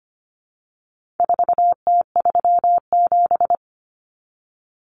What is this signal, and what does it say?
Morse code, heard as a keyed tone, about 25 wpm, 700 Hz. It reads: 4T37
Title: Common Cut Numbers - 4-Digit 25wpm